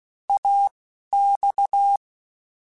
Morse code, heard as a keyed tone, about 16 wpm, spaced as for 12 wpm, 800 Hz. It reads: AX
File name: FL / FL mp3s / Fast FL @10